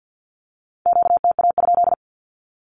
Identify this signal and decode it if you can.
Morse code, heard as a keyed tone, about 50 wpm, 700 Hz. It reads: QTA?
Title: Q-Codes with Spoken Meaning - 50wpm